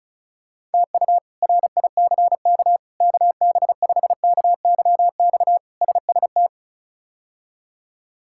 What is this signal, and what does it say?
Morse code, heard as a keyed tone, about 34 wpm, 700 Hz. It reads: TU RICK KB5KYX SST
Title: K1USN SST QSO Practice - 35wpm